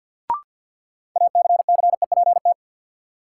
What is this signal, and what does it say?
Morse code, heard as a keyed tone, about 46 wpm, 700 Hz. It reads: ACCEPT
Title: Sentences from Top 1300 Words - New Word Drill - 50wpm